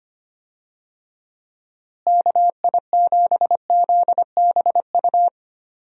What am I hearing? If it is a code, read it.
Morse code, heard as a keyed tone, about 25 wpm, 700 Hz. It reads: KI7ZBU
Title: USA Call Signs - 6 Characters - Repeated 3x - 25wpm